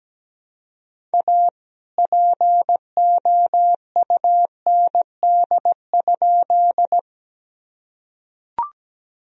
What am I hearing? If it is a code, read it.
Morse code, heard as a keyed tone, about 17 wpm, 700 Hz. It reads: A POUND?
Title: Sentences from Top 1000 Words - Review Part 2 - 17wpm